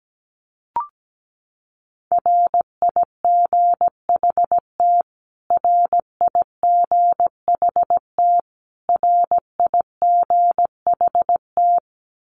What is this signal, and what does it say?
Morse code, heard as a keyed tone, about 17 wpm, 700 Hz. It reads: RIGHT RIGHT RIGHT
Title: Top 500 Words Repeated 17wpm